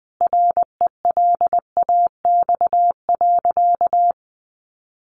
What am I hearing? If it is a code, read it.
Morse code, heard as a keyed tone, about 20 wpm, 700 Hz. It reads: RELAX.
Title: Sentences from Top 1400 Words 20wpm